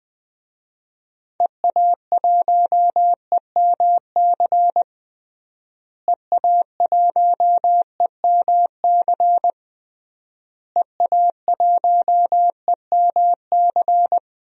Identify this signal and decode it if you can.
Morse code, heard as a keyed tone, about 20 wpm, 700 Hz. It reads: EA1EMC EA1EMC EA1EMC
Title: DXCC-100 Call Signs - 6 Characters - Repeated 3x - 20wpm